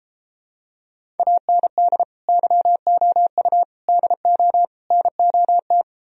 Morse code, heard as a keyed tone, about 33 wpm, 700 Hz. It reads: AND YOU DO NOT